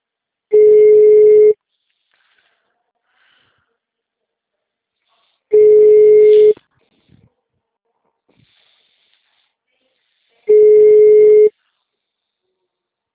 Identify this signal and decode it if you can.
Ringback tone (Europe, 425 Hz).
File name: out- -2007- - - .844